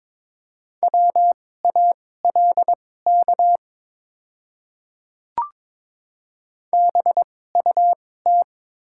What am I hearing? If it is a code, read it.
Morse code, heard as a keyed tone, about 22 wpm, spaced as for 16 wpm, 700 Hz. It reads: WALK BUT
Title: 2x Character Spacing - Top 500 Words - 22wpm